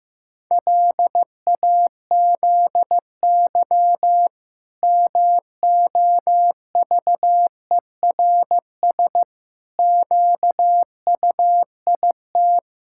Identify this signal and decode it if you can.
Morse code, heard as a keyed tone, about 15 wpm, 700 Hz. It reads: LAZY MOVERS QUIT
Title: Pangram Challenge v2 - 15wpm